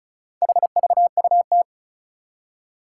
Morse code, heard as a keyed tone, about 36 wpm, 700 Hz. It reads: HVUT
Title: Four Challenging Letter-Number Combinations 35wpm